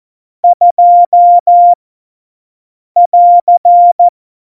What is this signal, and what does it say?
Morse code, heard as a keyed tone, about 14 wpm, 700 Hz. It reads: 2 <AR>
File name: Morsekurs / Witze / Witz 11 August 2020, 50 F70